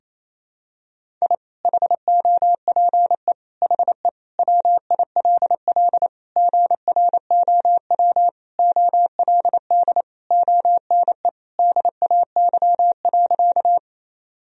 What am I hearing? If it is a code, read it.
Morse code, heard as a keyed tone, about 28 wpm, 700 Hz. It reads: I HOPE HE WILL GROW OLD ONE DAY.
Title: Sentences from Top 1000 Words - Review Part 1 - 28wpm